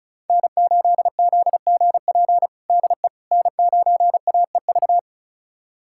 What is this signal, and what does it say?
Morse code, heard as a keyed tone, about 34 wpm, 690 Hz. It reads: N8ZGP DE N9AEV
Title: USA Callsign DE Callsign - 5 Characters - 35wpm